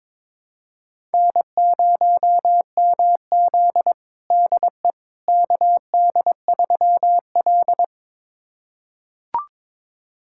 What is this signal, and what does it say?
Morse code, heard as a keyed tone, about 22 wpm, 700 Hz. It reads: N0MZ DE KD3L
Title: USA Callsign DE Callsign - 4 Characters - 22wpm